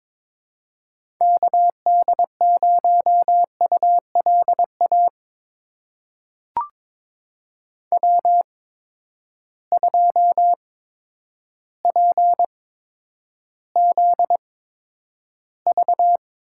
Morse code, heard as a keyed tone, about 22 wpm, spaced as for 10 wpm, 700 Hz. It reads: KD0ULA W2PZV
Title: USA Call Signs - 8x Character Spacing - 22wpm